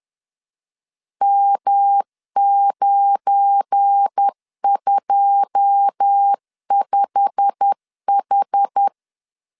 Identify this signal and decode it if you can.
Morse code, heard as a keyed tone, about 10 wpm, 800 Hz. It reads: M925H
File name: LEZIONI / 9 2 10wpm 2736